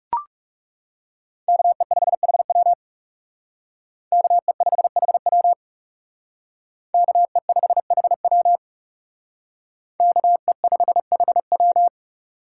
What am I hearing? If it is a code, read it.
Morse code, heard as a keyed tone, about 37 wpm, 700 Hz. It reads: KE5HW KE5HW KE5HW KE5HW
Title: Speed Racing - USA 2x2 Call Signs 30wpm